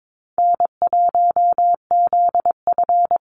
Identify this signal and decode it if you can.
Morse code, heard as a keyed tone, about 22 wpm, 700 Hz. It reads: N1ZF
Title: NA SOTA Activator Callsigns 22wpm